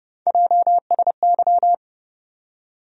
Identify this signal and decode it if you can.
Morse code, heard as a keyed tone, about 30 wpm, 700 Hz. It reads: JSY
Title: USA Call Sign Postfixes - 3 Letters 30wpm